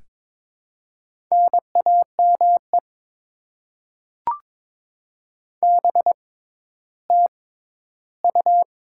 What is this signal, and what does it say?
Morse code, heard as a keyed tone, about 22 wpm, spaced as for 13 wpm, 700 Hz. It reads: NAME BTU
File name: Top 20 QSO Elements - 6x Character Spacing - 22wpm